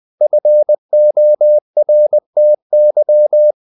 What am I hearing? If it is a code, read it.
Morse code, heard as a keyed tone, about 20 wpm, 600 Hz. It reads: FORTY
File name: the old man and the sea / the old man and the sea part 1 20wpm